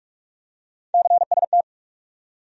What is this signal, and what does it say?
Morse code, heard as a keyed tone, about 44 wpm, 700 Hz. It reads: CST
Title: USA Call Sign Postfixes - 3 Letters 45wpm